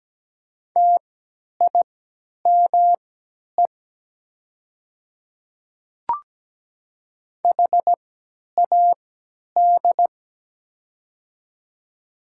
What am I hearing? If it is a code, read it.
Morse code, heard as a keyed tone, about 17 wpm, spaced as for 10 wpm, 700 Hz. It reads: TIME HAD
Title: 3x Character Spacing - Top 100 Words - 17wpm